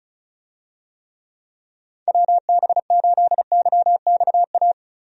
Morse code, heard as a keyed tone, about 34 wpm, 700 Hz. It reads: WB8YXA
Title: USA Call Signs - 6 Characters - Repeated 2x - 35wpm